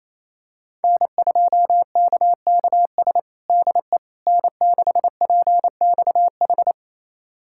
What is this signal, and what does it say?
Morse code, heard as a keyed tone, about 28 wpm, 700 Hz. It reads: N2KKS DE N6PXH
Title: USA Callsign DE Callsign - 5 Characters - 28wpm